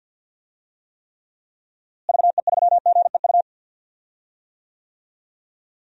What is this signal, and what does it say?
Morse code, heard as a keyed tone, about 51 wpm, 700 Hz. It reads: VE3GEU
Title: DXCC-100 Call Signs - 6 Characters - Repeated 3x - 50wpm